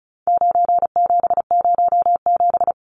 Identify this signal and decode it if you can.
Morse code, heard as a keyed tone, about 34 wpm, 700 Hz. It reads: 9707
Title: Numbers - 4-Digit 35wpm